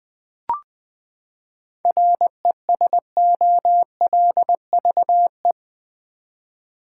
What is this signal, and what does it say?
Morse code, heard as a keyed tone, about 20 wpm, 700 Hz. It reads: RESOLVE
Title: Seven Letter Words - 20wpm